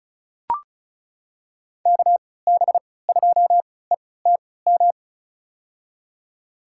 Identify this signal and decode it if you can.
Morse code, heard as a keyed tone, about 34 wpm, spaced as for 20 wpm, 700 Hz. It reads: KB2ETM
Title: USA Call Signs - 6 Characters - 3x Character Spacing - 35wpm